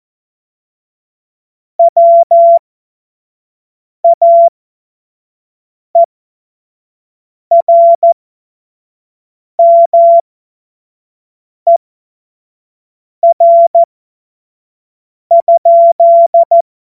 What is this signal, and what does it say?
Morse code, heard as a keyed tone, about 14 wpm, spaced as for 5 wpm, 680 Hz. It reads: WAERMER?
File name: Morsekurs / Witze / Witz 17. Mai 2022 25 F70